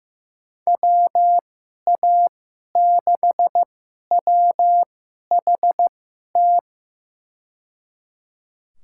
Morse code, heard as a keyed tone, about 15 wpm, spaced as for 11 wpm, 700 Hz. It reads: WA6WHT
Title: USA Call Signs - 6 Characters - 2x Character Spacing - 15wpm